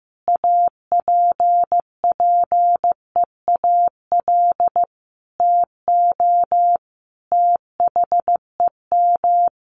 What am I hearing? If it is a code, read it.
Morse code, heard as a keyed tone, about 15 wpm, 700 Hz. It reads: APPEAL TO THEM
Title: Sentences from Top 1300 Words 15wpm